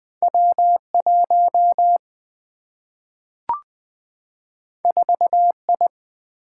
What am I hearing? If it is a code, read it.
Morse code, heard as a keyed tone, about 20 wpm, 700 Hz. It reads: W 1 4 I
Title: Single-Letter Single-Number 20wpm